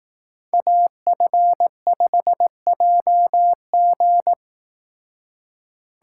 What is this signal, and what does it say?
Morse code, heard as a keyed tone, about 18 wpm, 700 Hz. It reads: AF5JG
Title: Speed Racing - USA 2x2 Call Signs 15wpm